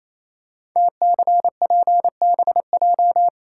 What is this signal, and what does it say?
Morse code, heard as a keyed tone, about 28 wpm, 700 Hz. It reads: TCPBJ
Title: Code Groups - 5 Letters - 28wpm